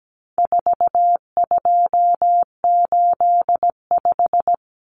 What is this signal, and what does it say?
Morse code, heard as a keyed tone, about 17 wpm, 700 Hz. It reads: 4285
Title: Common Cut Numbers - 4-Digit 17wpm